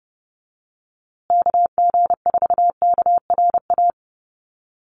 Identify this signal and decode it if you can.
Morse code, heard as a keyed tone, about 30 wpm, 700 Hz. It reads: KG4KRA
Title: USA Call Signs Repeated 2x 30wpm